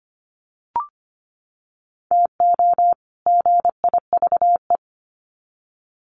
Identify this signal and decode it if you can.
Morse code, heard as a keyed tone, about 25 wpm, 700 Hz. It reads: TO GIVE
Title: Sets of 2 Words 25wpm